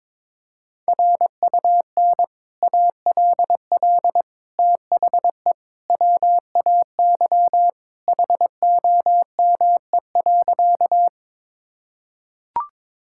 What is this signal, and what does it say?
Morse code, heard as a keyed tone, about 21 wpm, 700 Hz. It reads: RUN ALL THE WAY HOME.
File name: practice / 22wpm / Sentences with 3-Word Phrases - 22wpm